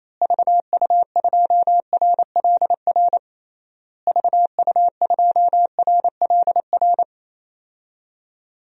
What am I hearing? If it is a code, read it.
Morse code, heard as a keyed tone, about 28 wpm, 700 Hz. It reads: VU2RLR VU2RLR
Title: DXCC-100 Call Signs - 6 Characters - Repeated 3x - 28wpm